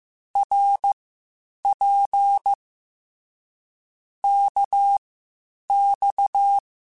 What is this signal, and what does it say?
Morse code, heard as a keyed tone, about 15 wpm, spaced as for 8 wpm, 800 Hz. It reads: RP KX
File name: Short Runs / Short Runs-mp3s / Balanced @13